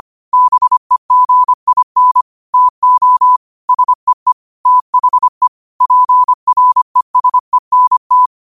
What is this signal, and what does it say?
Morse code, heard as a keyed tone, about 25 wpm, 1000 Hz. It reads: BEGIN TO SEE THE PRESENT
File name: qotdfiles / 25wpm